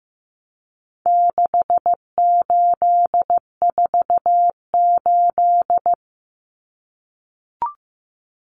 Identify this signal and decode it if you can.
Morse code, heard as a keyed tone, about 15 wpm, 700 Hz. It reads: 6848
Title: Numbers - 4-Digit 15wpm